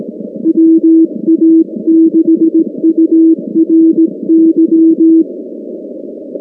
Morse code, heard as a keyed tone, about 18 wpm, 320 Hz. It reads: WA6URY
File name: contests / 2019-cw-wpx-kt5j-k5pi- -2330 0-19